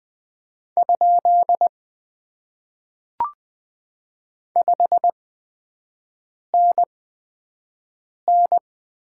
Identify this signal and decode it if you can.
Morse code, heard as a keyed tone, about 20 wpm, spaced as for 5 wpm, 700 Hz. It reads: ? 5NN
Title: Top 90 QSO Elements - 8x Character Spacing - 20wpm